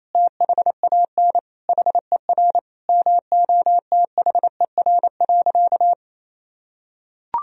Morse code, heard as a keyed tone, about 28 wpm, 700 Hz. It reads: THAN HER MOTHER.
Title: Sentences from Top 400 Words and Abbreviations 28wpm